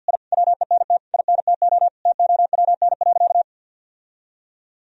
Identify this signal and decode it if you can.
Morse code, heard as a keyed tone, about 53 wpm, 700 Hz. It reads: I WENT INTO TOWN.
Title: Sentences from Top 1000 Words - Review Part 2 - 50wpm